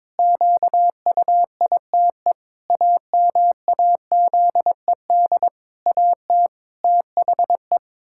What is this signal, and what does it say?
Morse code, heard as a keyed tone, about 22 wpm, 700 Hz. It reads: QUITE AMAZED AT THE
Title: Pangram Challenge v2 - 22wpm